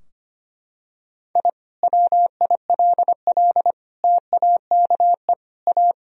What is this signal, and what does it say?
Morse code, heard as a keyed tone, about 25 wpm, 700 Hz. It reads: I WILL TAKE A
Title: Sentences from Top 1000 Words - Review Part 3 - 25wpm